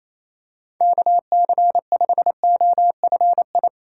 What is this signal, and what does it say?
Morse code, heard as a keyed tone, about 28 wpm, 700 Hz. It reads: KC5OFI
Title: USA Call Signs - 6 Characters - 4x Character Spacing - 28wpm